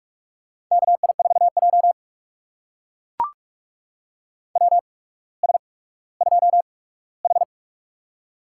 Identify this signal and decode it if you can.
Morse code, heard as a keyed tone, about 44 wpm, spaced as for 19 wpm, 690 Hz. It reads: KI4J WS2H
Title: USA Call Signs - 4 Characters - 8x Character Spacing - 45wpm